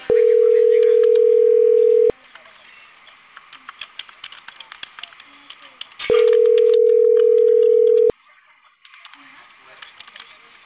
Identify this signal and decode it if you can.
Ringback tone (North America), 440 + 480 Hz, with speech.
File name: CENTRAL CAEA / out agcaea02 - .17440